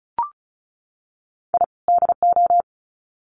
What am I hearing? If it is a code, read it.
Morse code, heard as a keyed tone, about 34 wpm, 700 Hz. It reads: I DO
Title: Sets of 2 Words 35wpm